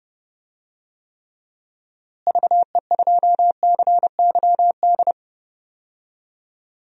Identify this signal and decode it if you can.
Morse code, heard as a keyed tone, about 30 wpm, 700 Hz. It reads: VE2CYD
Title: DXCC-100 Call Signs - 6 Characters - Repeated 2x - 30wpm